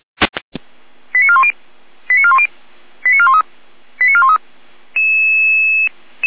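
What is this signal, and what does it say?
Five-tone selective call (ZVEI), heard as quick runs of five tones, tones of about 70 ms. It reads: 89410 89410 89413 89413
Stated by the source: ZVEI five-tone call 88076 received by SDR.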